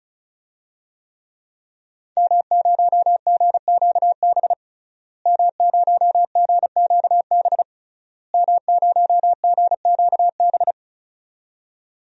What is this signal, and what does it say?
Morse code, heard as a keyed tone, about 36 wpm, 700 Hz. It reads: M0GQB M0GQB M0GQB
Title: DXCC-100 Call Signs - 5 Characters - Repeated 3x - 35wpm